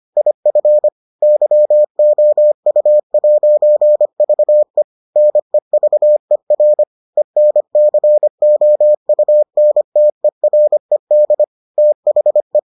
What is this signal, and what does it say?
Morse code, heard as a keyed tone, about 25 wpm, 600 Hz. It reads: IF YOU'VE NEVER ENCOUNTERED THE